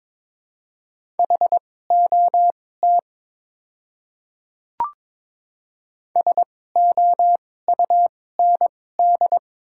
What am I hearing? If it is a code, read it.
Morse code, heard as a keyed tone, about 22 wpm, spaced as for 16 wpm, 690 Hz. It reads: HOT SOUND
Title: 2x Character Spacing - Top 100 Words - 22wpm